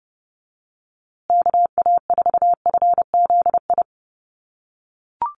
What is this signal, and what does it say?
Morse code, heard as a keyed tone, about 30 wpm, 700 Hz. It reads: KA4FZI
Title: USA Call Signs Repeated 2x 30wpm